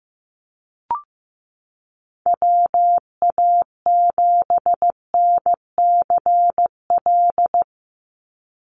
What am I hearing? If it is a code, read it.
Morse code, heard as a keyed tone, about 15 wpm, 700 Hz. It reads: WA7NCL
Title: NA SOTA Activator Callsigns 15wpm